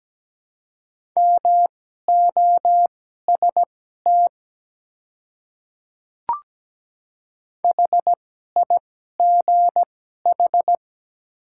Morse code, heard as a keyed tone, about 17 wpm, spaced as for 12 wpm, 700 Hz. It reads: MOST HIGH
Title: 2x Character Spacing - Top 200 Words - 17wpm